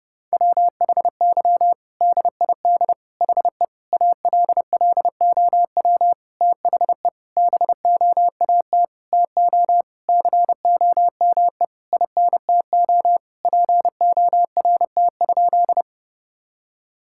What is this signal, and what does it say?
Morse code, heard as a keyed tone, about 30 wpm, 700 Hz. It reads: WHY DID HE ALLOW THE BOAT TO COME INTO PORT?
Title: Sentences from Top 1000 Words - Review Part 5 - 30wpm